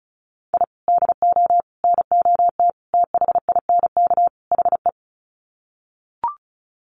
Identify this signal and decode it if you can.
Morse code, heard as a keyed tone, about 34 wpm, 700 Hz. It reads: I DO NOT THINK HE
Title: Sets of 5 Words 35wpm